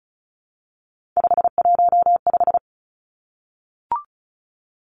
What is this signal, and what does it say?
Morse code, heard as a keyed tone, about 36 wpm, 700 Hz. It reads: 515
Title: Three Challenging Letter-Number Combinations 35wpm